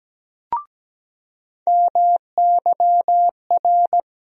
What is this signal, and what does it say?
Morse code, heard as a keyed tone, about 17 wpm, 700 Hz. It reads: MYR
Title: USA Call Sign Postfixes - 3 Letters 17wpm